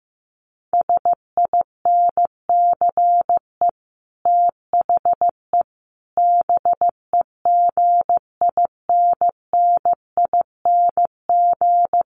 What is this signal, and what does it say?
Morse code, heard as a keyed tone, about 15 wpm, 700 Hz. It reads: SINCE THE BEGINNING
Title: Sets of 5 Words Encore2 15wpm